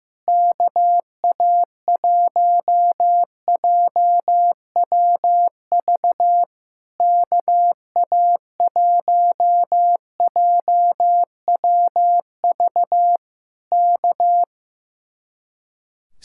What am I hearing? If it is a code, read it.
Morse code, heard as a keyed tone, about 15 wpm, 700 Hz. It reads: KA1JWV KA1JWV K
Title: Speed-Racing Common QSO Phrases Encore - 15wpm